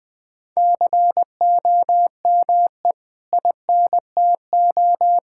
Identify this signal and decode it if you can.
Morse code, heard as a keyed tone, about 21 wpm, 700 Hz. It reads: COME INTO
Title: Sentences from Top 1000 Words - Review Part 6 - 20wpm